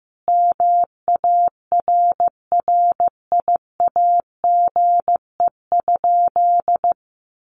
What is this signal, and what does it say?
Morse code, heard as a keyed tone, about 15 wpm, 700 Hz. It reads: MARRIAGE?
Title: Sentences from Top 1200 Words 15wpm